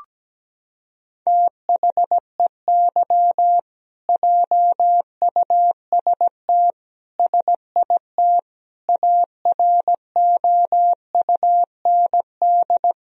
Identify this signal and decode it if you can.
Morse code, heard as a keyed tone, about 17 wpm, 700 Hz. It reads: THEY JUST SIT AROUND
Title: Sentences from Top 1000 Words - Review Part 3 - 17wpm